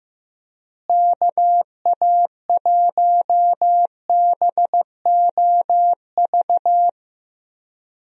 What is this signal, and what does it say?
Morse code, heard as a keyed tone, about 15 wpm, 700 Hz. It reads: KA1BOV
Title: USA Call Signs - 6 Characters - Repeated 3x - 15wpm